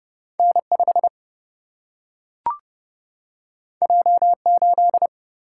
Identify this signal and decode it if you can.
Morse code, heard as a keyed tone, about 31 wpm, 700 Hz. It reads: N 5 J 8
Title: Single-Letter Single-Number 30wpm